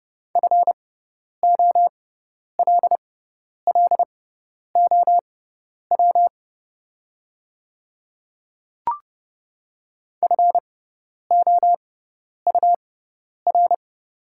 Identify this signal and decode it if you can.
Morse code, heard as a keyed tone, about 30 wpm, spaced as for 10 wpm, 700 Hz. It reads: FOLLOW FOUR
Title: 6x Character Spacing - Top 500 Words - 30wpm